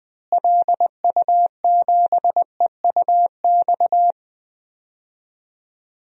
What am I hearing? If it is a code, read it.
Morse code, heard as a keyed tone, about 20 wpm, 700 Hz. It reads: LU7EUX
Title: DXCC-100 Call Signs - 6 Characters - Repeated 3x - 20wpm